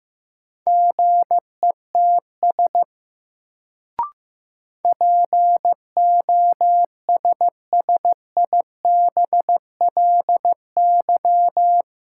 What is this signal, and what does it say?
Morse code, heard as a keyed tone, about 15 wpm, 700 Hz. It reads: GETS POSSIBLY
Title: Sentences from Top 1300 Words - New Word Drill - 15wpm